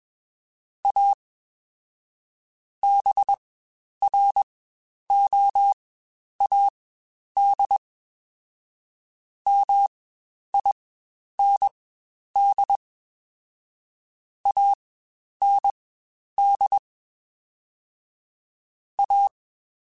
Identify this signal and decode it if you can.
Morse code, heard as a keyed tone, about 21 wpm, spaced as for 10 wpm, 800 Hz. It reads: A BROAD MIND AND A
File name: Spaced / Chuckles Spaced-mp3s / MiddleAge sp @20